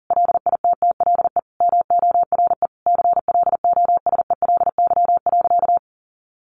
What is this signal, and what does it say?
Morse code, heard as a keyed tone, about 40 wpm, 700 Hz. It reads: LITTLE MORE CLOSELY.
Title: Sentences from Top 1400 Words 40wpm